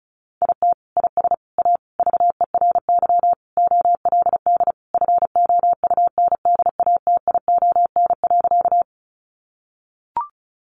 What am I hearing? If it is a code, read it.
Morse code, heard as a keyed tone, about 34 wpm, 700 Hz. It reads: IT IS A VERY OLD FOUNDATION.
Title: Sentences from Top 1200 Words 35wpm